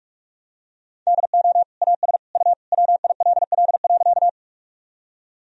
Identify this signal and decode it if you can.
Morse code, heard as a keyed tone, about 45 wpm, 700 Hz. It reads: DO AS U WILL.
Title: Sentences from Top 100 Words and Abbreviations 45wpm